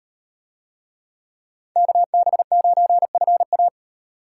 Morse code, heard as a keyed tone, about 38 wpm, 700 Hz. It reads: KB9FA